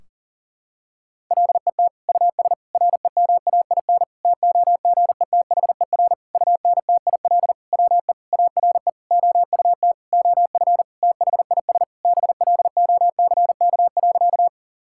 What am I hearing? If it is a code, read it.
Morse code, heard as a keyed tone, about 40 wpm, 700 Hz. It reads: LET US REMAIN TOGETHER UNTIL WE ARE OUT OF THIS BLOCK.
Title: Sentences from Top 1600 Words 40wpm